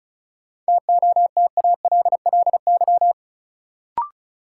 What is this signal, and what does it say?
Morse code, heard as a keyed tone, about 36 wpm, 700 Hz. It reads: TOTALLY
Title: Sentences from Top 1600 Words - New Word Drill - 35wpm